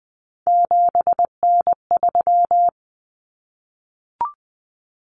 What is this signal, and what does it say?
Morse code, heard as a keyed tone, about 20 wpm, 700 Hz. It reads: 7N3
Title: Common Cut Numbers - 3-Digit 20wpm